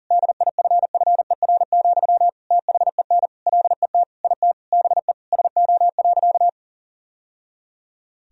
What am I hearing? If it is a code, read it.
Morse code, heard as a keyed tone, about 40 wpm, 700 Hz. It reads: DIFFER, THEN LET IT BE SO.
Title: Sentences from Top 1000 Words - Review Part 1 - 40wpm